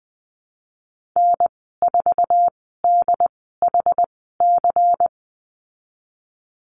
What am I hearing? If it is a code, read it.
Morse code, heard as a keyed tone, about 20 wpm, spaced as for 15 wpm, 700 Hz. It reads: N4DHC
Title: 193 Lesson 31g - Letter v - Call Signs 20wpm